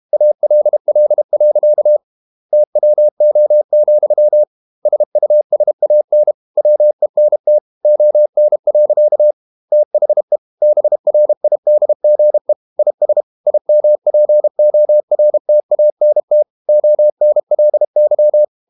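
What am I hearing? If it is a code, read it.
Morse code, heard as a keyed tone, about 32 wpm, 600 Hz. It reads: ALL. TWO, SUSAN WENT ON. THE BRIDGE IS IMPORTANT ONLY
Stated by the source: Morse code practice at 32 wpm from CW med Gnister, reading Ken Follett's novel Never in English.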